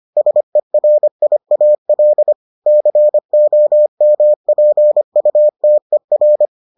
Morse code, heard as a keyed tone, about 25 wpm, 600 Hz. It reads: SERIAL COMPUTER